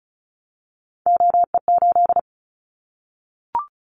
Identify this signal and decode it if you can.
Morse code, heard as a keyed tone, about 34 wpm, 700 Hz. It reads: OE8
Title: Callsign Prefixes for Top 25 DXCC - 35wpm